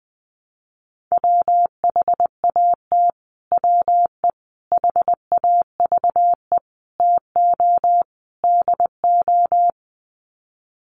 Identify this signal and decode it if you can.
Morse code, heard as a keyed tone, about 20 wpm, 700 Hz. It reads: WHAT WE HAVE TO DO
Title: Sets of 5 Words Encore 20wpm